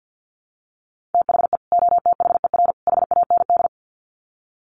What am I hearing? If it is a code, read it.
Morse code, heard as a keyed tone, about 48 wpm, 700 Hz. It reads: THE OTHER HAND
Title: Sets of 3 Words 50wpm